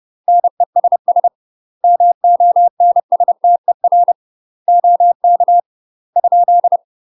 Morse code, heard as a keyed tone, about 30 wpm, 700 Hz. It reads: NESS MONSTER OK ?
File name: UK MASTER QSOs TRAINING / 30WPM / UK MASTER QSOs 0050